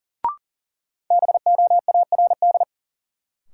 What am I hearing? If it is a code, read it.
Morse code, heard as a keyed tone, about 37 wpm, 700 Hz. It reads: BOARD
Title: Sentences from Top 700 Words - New Word Drill - 40wpm